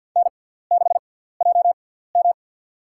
Morse code, heard as a keyed tone, about 53 wpm, spaced as for 17 wpm, 700 Hz. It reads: N6JM
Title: USA Call Signs - 6x Character Spacing - 50wpm